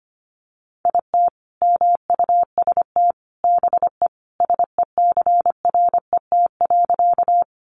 Morse code, heard as a keyed tone, about 25 wpm, 700 Hz. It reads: IT MUST BE SECRET.
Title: Sentences from Top 1200 Words 25wpm